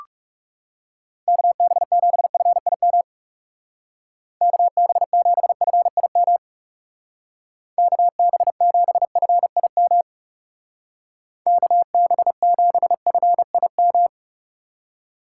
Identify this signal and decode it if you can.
Morse code, heard as a keyed tone, about 36 wpm, 700 Hz. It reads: KB7FIM KB7FIM KB7FIM KB7FIM
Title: Speed Racing - USA 2x3 Call Signs 30wpm